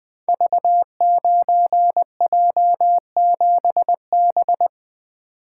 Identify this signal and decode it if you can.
Morse code, heard as a keyed tone, about 21 wpm, 700 Hz. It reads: V9J7B
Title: Code Groups - 5 Letter:Numbers - 20wpm